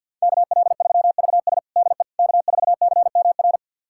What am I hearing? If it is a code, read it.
Morse code, heard as a keyed tone, about 49 wpm, 700 Hz. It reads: KL3VS DE K4CMR
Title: USA Callsign DE Callsign - 5 Characters - 50wpm